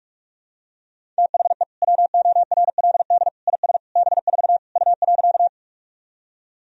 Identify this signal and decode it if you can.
Morse code, heard as a keyed tone, about 46 wpm, 700 Hz. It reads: THE WORLD IS B4 U.
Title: Sentences from Top 200 Words and Abbreviations 45wpm